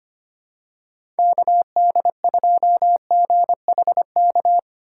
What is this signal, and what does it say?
Morse code, heard as a keyed tone, about 25 wpm, 700 Hz. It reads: KD2GHK
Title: USA Call Signs - 2x Character Spacing - 25wpm